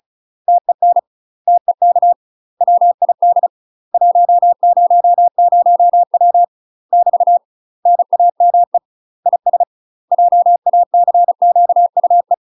Morse code, heard as a keyed tone, about 35 wpm, 700 Hz. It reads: TEN TEK WID 100W <BT> NAME IS JACQUE
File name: UK MASTER QSOs TRAINING / 35WPM / UK MASTER QSOs 0012